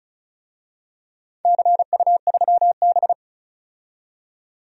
Morse code, heard as a keyed tone, about 36 wpm, 700 Hz. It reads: CU3B
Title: DXCC-100 Call Signs - 4 Characters - 35wpm